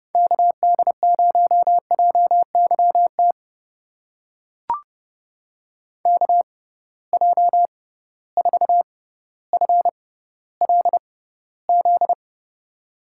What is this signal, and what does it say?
Morse code, heard as a keyed tone, about 30 wpm, spaced as for 15 wpm, 700 Hz. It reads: KD0JYT KJ4FLZ
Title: USA Call Signs - 6 Characters - 6x Character Spacing - 30wpm